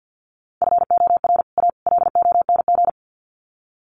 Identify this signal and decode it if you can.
Morse code, heard as a keyed tone, about 50 wpm, 700 Hz. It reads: FOR A LONG
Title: Sets of 3 Words 50wpm